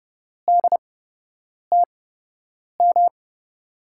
Morse code, heard as a keyed tone, about 31 wpm, spaced as for 8 wpm, 700 Hz. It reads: DTM
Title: USA Call Signs - 6 Characters - 8x Character Spacing - 30wpm